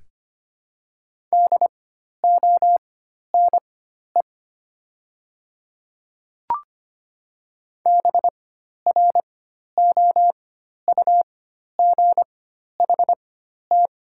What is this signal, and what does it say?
Morse code, heard as a keyed tone, about 25 wpm, spaced as for 12 wpm, 700 Hz. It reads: DONE BROUGHT
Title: 4x Character Spacing - Top 500 Words - 25wpm